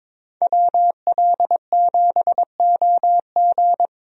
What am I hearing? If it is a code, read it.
Morse code, heard as a keyed tone, about 22 wpm, 700 Hz. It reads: WL7OG
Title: USA Call Signs - 5 Characters - Repeated 2x - 22wpm